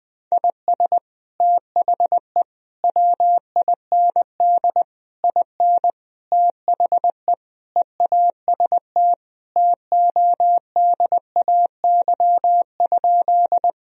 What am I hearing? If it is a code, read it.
Morse code, heard as a keyed tone, about 20 wpm, 700 Hz. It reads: IS THE WIND IN THE EAST TODAY?
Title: Sentences from Top 1000 Words - Review Part 3 - 20wpm